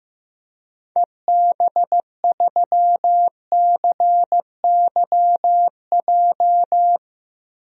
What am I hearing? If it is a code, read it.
Morse code, heard as a keyed tone, about 15 wpm, 700 Hz. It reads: EB3CYJ
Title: DXCC-100 Call Signs - 6 Characters - Repeated 2x - 15wpm